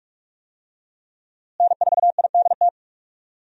Morse code, heard as a keyed tone, about 42 wpm, 700 Hz. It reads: N4IDT
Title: USA Call Signs - 5 Characters - Repeated 3x - 45wpm